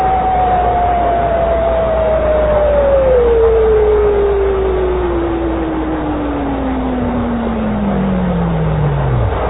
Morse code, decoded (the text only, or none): none